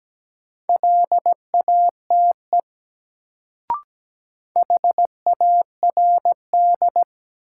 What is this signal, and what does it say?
Morse code, heard as a keyed tone, about 17 wpm, 700 Hz. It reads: LATE HARD
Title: Sentences from Top 300 Words - New Word Drill - 17wpm